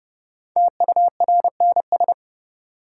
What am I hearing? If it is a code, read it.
Morse code, heard as a keyed tone, about 32 wpm, 700 Hz. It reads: TURNS
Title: Sets of 5 Words Encore - New Word Drill - 30wpm